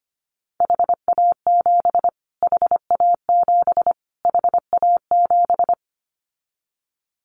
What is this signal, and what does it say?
Morse code, heard as a keyed tone, about 25 wpm, 700 Hz. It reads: HA7 HA7 HA7
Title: Callsign Prefixes for Top 25 DXCC - Repeated 3x - 25wpm